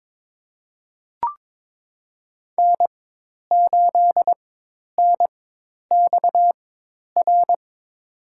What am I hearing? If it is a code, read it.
Morse code, heard as a keyed tone, about 21 wpm, spaced as for 10 wpm, 700 Hz. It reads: N8NXR
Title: USA Call Signs - 5 Characters - 4x Character Spacing - 22wpm